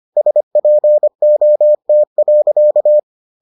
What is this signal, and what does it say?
Morse code, heard as a keyed tone, about 25 wpm, 600 Hz. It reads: SPOT.